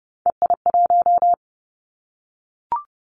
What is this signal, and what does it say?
Morse code, heard as a keyed tone, about 30 wpm, 700 Hz. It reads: EI1
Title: Callsign Prefixes for Top 50 DXCC - 30wpm